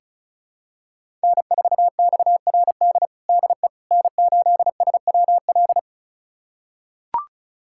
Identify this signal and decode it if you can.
Morse code, heard as a keyed tone, about 34 wpm, 700 Hz. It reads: N4XRD DE N8SWL
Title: USA Callsign DE Callsign 35wpm